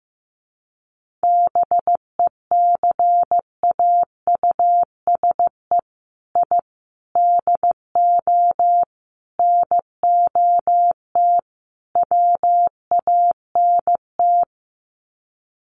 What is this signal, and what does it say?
Morse code, heard as a keyed tone, about 15 wpm, 700 Hz. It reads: BECAUSE I DO NOT WANT
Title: Sets of 5 Words Encore 15wpm